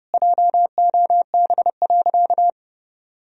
Morse code, heard as a keyed tone, about 32 wpm, 700 Hz. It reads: JOB.
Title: Sentences from Top 1000 Words - Review Part 3 - 30wpm